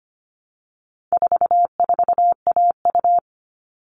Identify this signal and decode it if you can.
Morse code, heard as a keyed tone, about 25 wpm, 700 Hz. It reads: 44AU
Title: All Cut Numbers - 4-Digit 25wpm